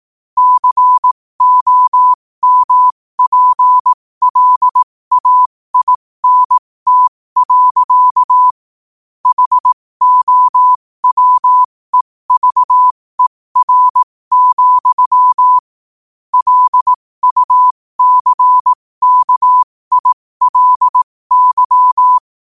Morse code, heard as a keyed tone, about 18 wpm, spaced as for 15 wpm, 1000 Hz. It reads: COMPLAINT. HOWEVER, LUCKILY
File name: qotdfiles / 15wpm